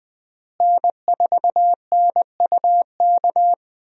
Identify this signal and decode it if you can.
Morse code, heard as a keyed tone, about 20 wpm, 700 Hz. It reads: N4NUK
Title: USA Call Signs - 5 Characters - Repeated 2x - 20wpm